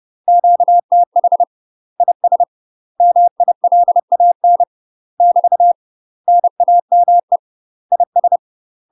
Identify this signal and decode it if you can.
Morse code, heard as a keyed tone, about 30 wpm, 700 Hz. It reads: QTH IS MILAN <BT> NAME IS
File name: UK MASTER QSOs TRAINING / 30WPM / UK MASTER QSOs 0007